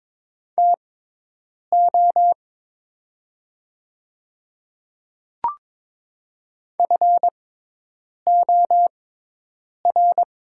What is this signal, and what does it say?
Morse code, heard as a keyed tone, about 22 wpm, spaced as for 8 wpm, 700 Hz. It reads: TO FOR
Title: 6x Character Spacing - Top 200 Words - 22wpm